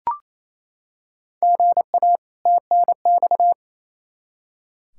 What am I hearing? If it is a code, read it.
Morse code, heard as a keyed tone, about 28 wpm, 700 Hz. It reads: GA TNX
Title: Practice QSO - Exchange 1 - Beginning - 28wpm